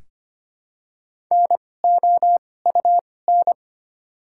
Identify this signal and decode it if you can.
Morse code, heard as a keyed tone, about 25 wpm, spaced as for 18 wpm, 700 Hz. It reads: NOUN
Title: 2x Character Spacing - Top 500 Words - 25wpm